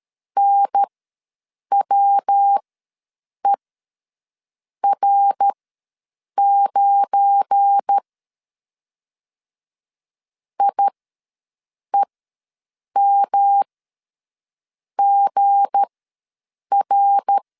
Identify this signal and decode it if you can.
Morse code, heard as a keyed tone, about 13 wpm, spaced as for 6 wpm, 800 Hz. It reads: NWER9 IEMGR
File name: LEZIONI / 7 GF9Z / 7 GF9 ws 12wpm 3344